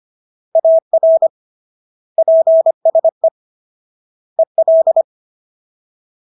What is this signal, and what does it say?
Morse code, heard as a keyed tone, about 25 wpm, 650 Hz. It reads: AR PSE EL